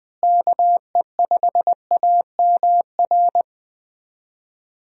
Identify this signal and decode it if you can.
Morse code, heard as a keyed tone, about 20 wpm, 700 Hz. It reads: KE5AMR
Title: USA Call Signs - 6 Characters - Repeated 3x - 20wpm